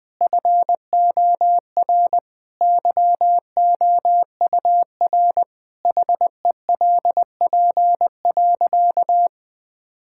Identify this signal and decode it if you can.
Morse code, heard as a keyed tone, about 20 wpm, 700 Hz. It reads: FOR YOUR HELP.
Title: Sentences with 5-Word Phrases - 20wpm